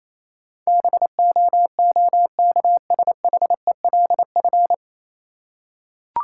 Morse code, heard as a keyed tone, about 28 wpm, 700 Hz. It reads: BOOKSHELF
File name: Easy Words - Speed-Racing with Hints - 22wpm